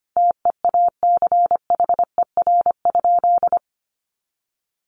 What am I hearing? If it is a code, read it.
Morse code, heard as a keyed tone, about 25 wpm, 700 Hz. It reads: TEACHER?
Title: Sentences from Top 1300 Words 25wpm